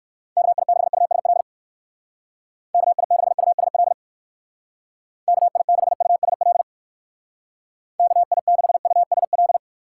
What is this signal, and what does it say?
Morse code, heard as a keyed tone, about 65 wpm, 700 Hz. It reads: KI6USL KI6USL KI6USL KI6USL